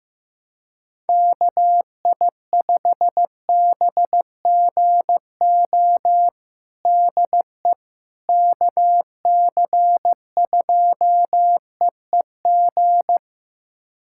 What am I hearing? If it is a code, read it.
Morse code, heard as a keyed tone, about 15 wpm, 700 Hz. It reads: KI5BGO DE KC2EEG
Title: USA Callsign DE Callsign 15wpm